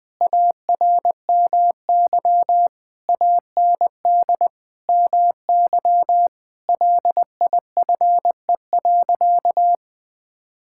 Morse code, heard as a keyed tone, about 20 wpm, 700 Hz. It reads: ARMY AND MY LIFE.
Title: Sentences from Top 1000 Words - Review Part 5 - 20wpm